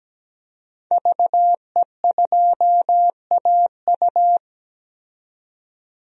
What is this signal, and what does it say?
Morse code, heard as a keyed tone, about 17 wpm, 700 Hz. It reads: VE2AU
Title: DXCC-100 Call Signs - 5 Characters - 17wpm